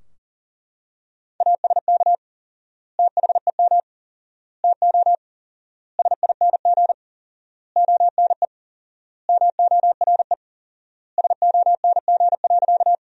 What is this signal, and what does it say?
Morse code, heard as a keyed tone, about 40 wpm, 700 Hz. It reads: ASK THEM TO SING ONE MORE SONG.
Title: 4x Word-Spacing - Sentences from Top 400 Words - 40wpm